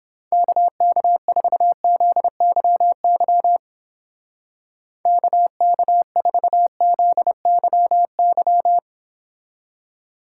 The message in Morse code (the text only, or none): KK4ZYY KK4ZYY